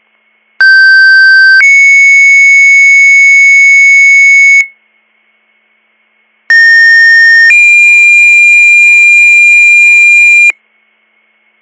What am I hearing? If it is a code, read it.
Two-tone page, heard as a one-second tone, then a three-second tone, about 1550 then 2200 Hz, about 1850 then 2400 Hz. It is heard from 0.6 to 10.5 s.